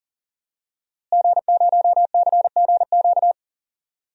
Morse code, heard as a keyed tone, about 40 wpm, 700 Hz. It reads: G0CGQ
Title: DXCC-100 Call Signs - 5 Characters - Repeated 3x - 40wpm